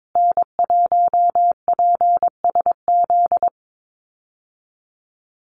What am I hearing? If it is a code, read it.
Morse code, heard as a keyed tone, about 22 wpm, 700 Hz. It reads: N1PSZ
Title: USA Call Signs Repeated 3x 22wpm